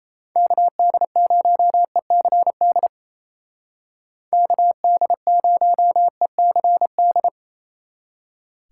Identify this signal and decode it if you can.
Morse code, heard as a keyed tone, about 35 wpm, 700 Hz. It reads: KD0ECD KD0ECD